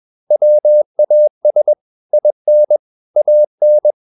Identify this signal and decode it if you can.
Morse code, heard as a keyed tone, about 21 wpm, 600 Hz. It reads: WAS IN AN
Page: CW med Gnister